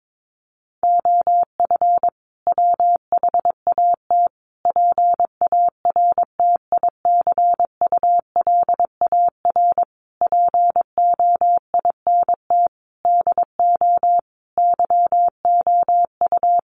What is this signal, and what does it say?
Morse code, heard as a keyed tone, about 22 wpm, 700 Hz. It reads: OF WHAT PARTICULAR POINT DO YOU
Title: Sentences from Top 1300 Words 22wpm